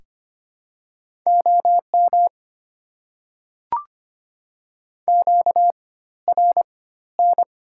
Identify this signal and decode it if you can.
Morse code, heard as a keyed tone, about 25 wpm, spaced as for 14 wpm, 700 Hz. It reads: OM QRN
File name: Top 90 QSO Elements - 4x Character Spacing - 25wpm